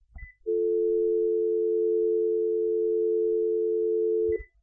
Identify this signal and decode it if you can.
Dial tone (North America), 350 + 440 Hz.